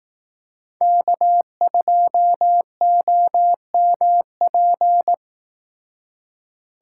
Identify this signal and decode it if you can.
Morse code, heard as a keyed tone, about 18 wpm, 700 Hz. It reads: K2OMP
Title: Speed Racing - USA 1x3 Call Signs 15wpm